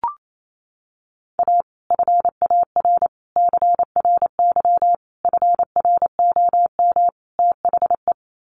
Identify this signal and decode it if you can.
Morse code, heard as a keyed tone, about 28 wpm, 700 Hz. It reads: A FAR CRY FROM THE
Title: Sets of 5 Words Encore 28wpm